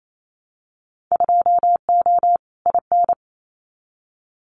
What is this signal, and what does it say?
Morse code, heard as a keyed tone, about 28 wpm, 700 Hz. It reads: 2O IN
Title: Winter Field Day Exchanges 28wpm